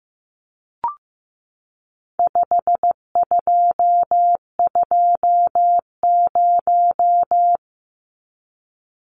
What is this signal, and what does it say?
Morse code, heard as a keyed tone, about 15 wpm, 700 Hz. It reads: 5220
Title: Numbers - 4-Digit 15wpm